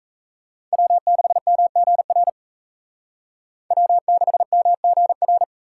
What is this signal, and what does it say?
Morse code, heard as a keyed tone, about 36 wpm, 700 Hz. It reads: W6MGR W6MGR